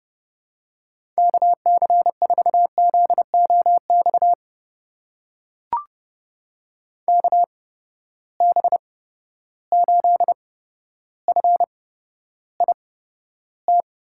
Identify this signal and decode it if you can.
Morse code, heard as a keyed tone, about 30 wpm, spaced as for 13 wpm, 700 Hz. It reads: KC4ZOX KB8FIT
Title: USA Call Signs - 6 Characters - 8x Character Spacing - 30wpm